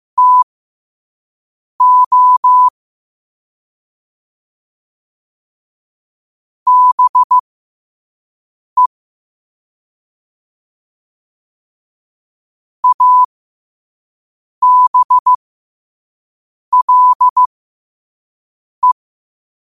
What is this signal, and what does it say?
Morse code, heard as a keyed tone, about 15 wpm, spaced as for 5 wpm, 1000 Hz. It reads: TO BE ABLE
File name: qotdfiles / 5wpm